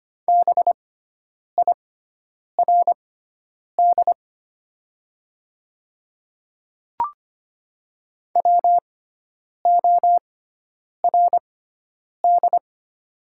Morse code, heard as a keyed tone, about 25 wpm, spaced as for 9 wpm, 690 Hz. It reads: BIRD WORD
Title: 6x Character Spacing - Top 500 Words - 25wpm